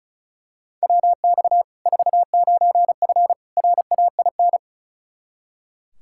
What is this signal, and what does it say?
Morse code, heard as a keyed tone, about 36 wpm, 700 Hz. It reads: WX 49F RAIN
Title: Practice QSO - Exchange 2 - Info - 35wpm